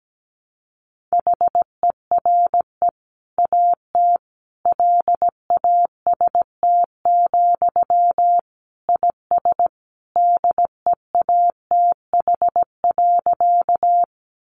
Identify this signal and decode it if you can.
Morse code, heard as a keyed tone, about 17 wpm, 700 Hz. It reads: HERE AT LAST, IS DEATH.
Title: Sentences from Top 1100 Words 17wpm